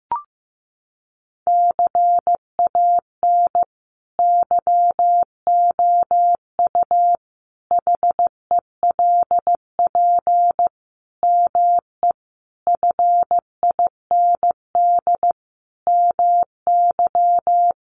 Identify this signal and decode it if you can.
Morse code, heard as a keyed tone, about 15 wpm, 700 Hz. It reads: CAN YOU HELP ME FIND MY
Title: Sentences from Top 200 Words 15wpm